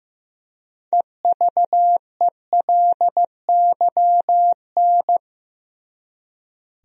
Morse code, heard as a keyed tone, about 16 wpm, 700 Hz. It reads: EVELYN
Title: USA Common Names - 15wpm